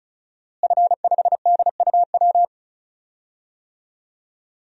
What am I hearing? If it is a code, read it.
Morse code, heard as a keyed tone, about 36 wpm, 700 Hz. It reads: F5DUW
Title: DXCC-100 Call Signs - 5 Characters - Repeated 2x - 35wpm